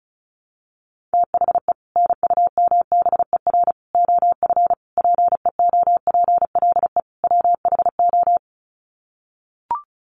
Morse code, heard as a keyed tone, about 34 wpm, 700 Hz. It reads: THE NUMBER OF PEOPLE WHO
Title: Sets of 5 Words Encore 35wpm